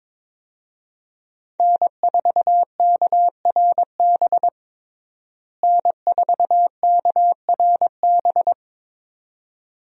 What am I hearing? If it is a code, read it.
Morse code, heard as a keyed tone, about 22 wpm, 700 Hz. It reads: N4KRB N4KRB
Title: USA Call Signs - 5 Characters - Repeated 2x - 22wpm